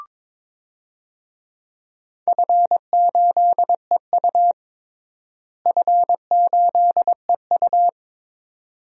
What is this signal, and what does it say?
Morse code, heard as a keyed tone, about 22 wpm, 700 Hz. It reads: F8EU F8EU
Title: DXCC-100 Call Signs - 4 Characters - Repeated 2x - 22wpm